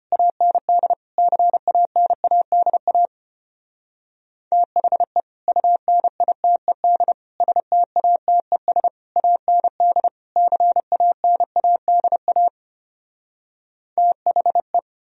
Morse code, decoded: AND CANADA THE UNITED STATES AND CANADA THE

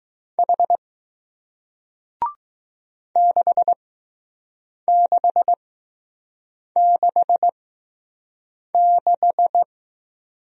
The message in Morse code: H 6666

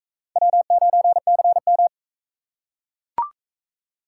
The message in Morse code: W9CM

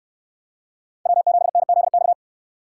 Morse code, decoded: W7NZX

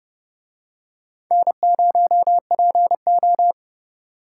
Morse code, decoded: N0PO

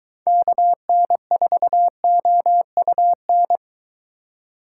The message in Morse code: KN4OUN

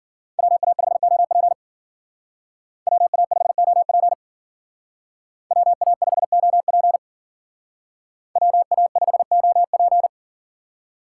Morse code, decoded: WA5OP WA5OP WA5OP WA5OP

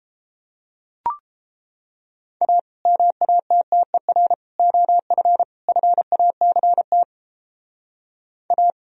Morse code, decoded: A MATTER OF FACT A

